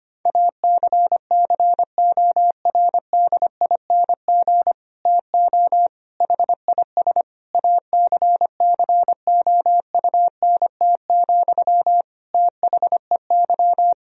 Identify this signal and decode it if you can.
Morse code, heard as a keyed tone, about 25 wpm, 700 Hz. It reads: ACCORDING TO HIS ACCOUNT, THEY
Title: Sentences from Top 1000 Words - Review Part 4 - 25wpm